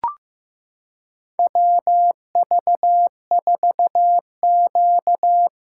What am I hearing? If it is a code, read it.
Morse code, heard as a keyed tone, about 15 wpm, 700 Hz. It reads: WV4Q
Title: USA Callsign DE Callsign - 4 Characters - 15wpm